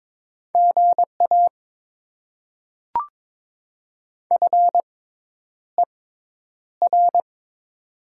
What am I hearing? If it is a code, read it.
Morse code, heard as a keyed tone, about 22 wpm, spaced as for 10 wpm, 700 Hz. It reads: GA FER
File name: Top 20 QSO Elements - 6x Character Spacing - 22wpm